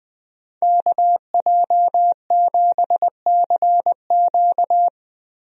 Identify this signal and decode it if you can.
Morse code, heard as a keyed tone, about 21 wpm, 700 Hz. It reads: KJ7CQ
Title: USA Call Signs - 5 Characters - Repeated 3x - 20wpm